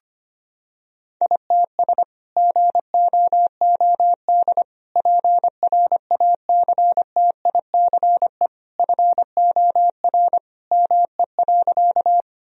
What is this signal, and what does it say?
Morse code, heard as a keyed tone, about 25 wpm, 700 Hz. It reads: ITS GOOD PRACTICE FOR ME.
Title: Sentences from Top 1000 Words - Review Part 4 - 25wpm